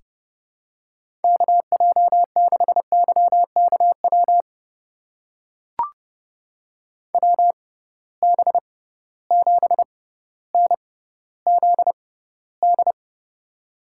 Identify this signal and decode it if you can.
Morse code, heard as a keyed tone, about 30 wpm, spaced as for 15 wpm, 700 Hz. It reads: KJ6YKW WB7NZD
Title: USA Call Signs - 6 Characters - 6x Character Spacing - 30wpm